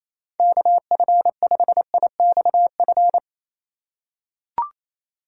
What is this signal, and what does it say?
Morse code, heard as a keyed tone, about 28 wpm, 700 Hz. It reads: KF5IXF
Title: USA Call Signs - 6 Characters - 2x Character Spacing - 28wpm